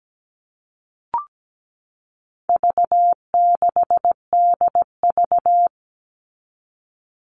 Morse code, heard as a keyed tone, about 17 wpm, 700 Hz. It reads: V6DV
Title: All Cut Numbers - 4-Digit 17wpm